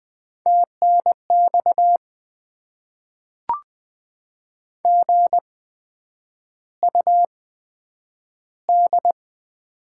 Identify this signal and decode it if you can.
Morse code, heard as a keyed tone, about 20 wpm, spaced as for 9 wpm, 700 Hz. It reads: TNX GUD
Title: Top 20 QSO Elements - 8x Character Spacing - 20wpm